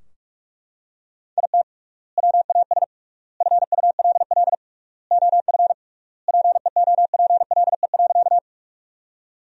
Morse code, heard as a keyed tone, about 41 wpm, 700 Hz. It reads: IT WAS FULL OF PEOPLE.
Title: 3x Word-Spacing - Sentences from Top 500 Words - 45wpm